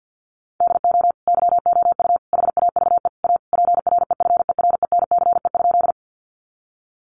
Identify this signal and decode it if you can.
Morse code, heard as a keyed tone, about 50 wpm, 700 Hz. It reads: DO YOU HAVE A PREFERENCE?
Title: Sentences from Top 1500 Words 50wpm